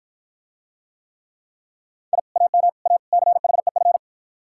Morse code, heard as a keyed tone, about 47 wpm, 700 Hz. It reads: I AM A CHEF